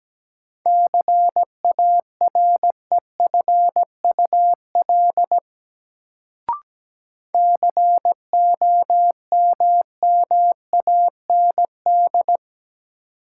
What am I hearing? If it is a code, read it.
Morse code, heard as a keyed tone, about 17 wpm, 700 Hz. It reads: CAREFUL COMMAND
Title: Sentences from Top 1600 Words - New Word Drill Alt - 17wpm